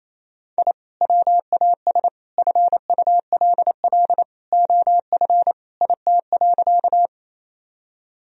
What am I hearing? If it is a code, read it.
Morse code, heard as a keyed tone, about 28 wpm, 700 Hz. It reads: I WAS FULL OF IT.
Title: Sentences from Top 1000 Words - Review Part 3 - 28wpm